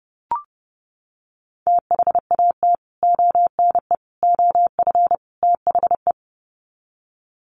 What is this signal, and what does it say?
Morse code, heard as a keyed tone, about 30 wpm, 700 Hz. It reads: THAT ONE OF THE
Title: Sets of 4 Words 30wpm